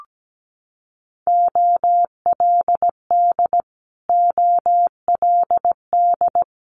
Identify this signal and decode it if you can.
Morse code, heard as a keyed tone, about 17 wpm, 700 Hz. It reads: OLD OLD
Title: Multi-letter Phonemes Repeated 3x 17wpm